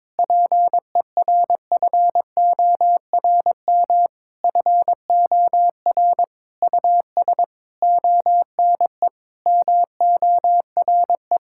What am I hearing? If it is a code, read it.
Morse code, heard as a keyed tone, about 22 wpm, 700 Hz. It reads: PERFORM FOR US ONE MORE
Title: Sentences from Top 1000 Words - Review Part 5 - 22wpm